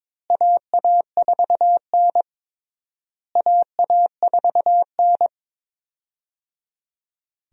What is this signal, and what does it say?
Morse code, heard as a keyed tone, about 22 wpm, 700 Hz. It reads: AA4N AA4N
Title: USA Call Signs - 4 Characters - Repeated 3x - 22wpm